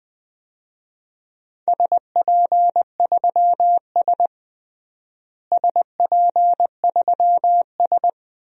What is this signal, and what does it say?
Morse code, heard as a keyed tone, about 20 wpm, 700 Hz. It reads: SP3S SP3S
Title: DXCC-100 Call Signs - 4 Characters - Repeated 2x - 20wpm